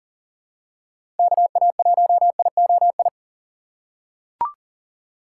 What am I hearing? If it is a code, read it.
Morse code, heard as a keyed tone, about 40 wpm, 700 Hz. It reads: KA1IOI
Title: USA Call Signs - 6 Characters - 3x Character Spacing - 40wpm